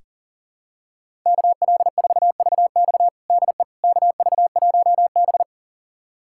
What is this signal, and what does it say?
Morse code, heard as a keyed tone, about 40 wpm, 700 Hz. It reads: KL4VX DE KV1B
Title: USA Callsign DE Callsign 40wpm